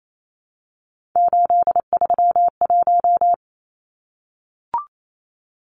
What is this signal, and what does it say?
Morse code, heard as a keyed tone, about 28 wpm, 700 Hz. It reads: 831
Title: Numbers - 3-Digit 28wpm